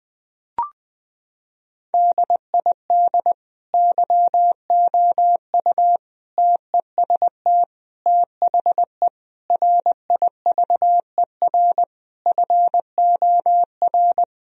Morse code, heard as a keyed tone, about 21 wpm, 700 Hz. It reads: DID YOU TEST THE RIVER FOR
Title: Sentences from Top 1000 Words - Review Part 4 - 20wpm